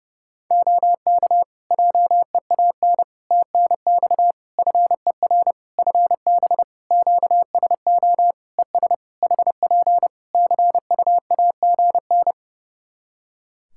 Morse code, heard as a keyed tone, about 30 wpm, 700 Hz. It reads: OK JEAN TNX FER FB QSO ES HP CUAGN
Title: Practice QSO - Exchange 4 - Ending - 30wpm